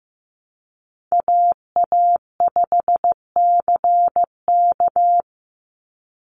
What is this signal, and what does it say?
Morse code, heard as a keyed tone, about 15 wpm, 700 Hz. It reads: AA5CK
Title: NA SOTA Chaser Callsigns 15wpm